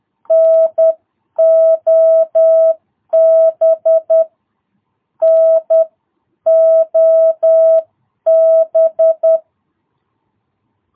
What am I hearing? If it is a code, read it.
Morse code, heard as a keyed tone, about 10 wpm, spaced as for 9 wpm, 650 Hz. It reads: NOB NOB
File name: jh8cbh / 2022cw